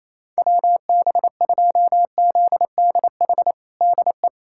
Morse code, heard as a keyed tone, about 28 wpm, 700 Hz. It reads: WB2ZDH DE
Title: USA Callsign DE Callsign 28wpm